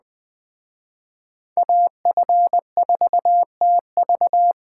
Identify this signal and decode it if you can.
Morse code, heard as a keyed tone, about 24 wpm, 700 Hz. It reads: AF4TV
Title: Speed Racing - USA 2x2 Call Signs 20wpm